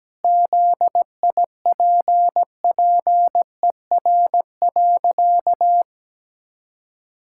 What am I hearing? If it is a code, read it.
Morse code, heard as a keyed tone, about 17 wpm, 700 Hz. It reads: ZIPPER.
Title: Pangram Challenge v2 - 17wpm